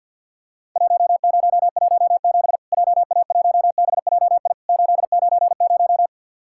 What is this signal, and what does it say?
Morse code, heard as a keyed tone, about 53 wpm, 700 Hz. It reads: 1017 JA1BJI 890
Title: Worked All Europe DX Contest - QTC Exchange - 50wpm